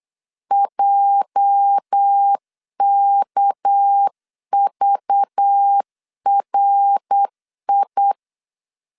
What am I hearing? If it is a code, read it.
Morse code, heard as a keyed tone, about 8 wpm, 800 Hz. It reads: JKVRI
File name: LEZIONI / 9 286 8wpm 3425